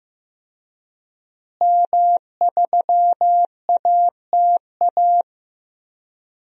Morse code, heard as a keyed tone, about 15 wpm, 700 Hz. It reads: M3ATA
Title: DXCC-100 Call Signs - 5 Characters - Repeated 3x - 15wpm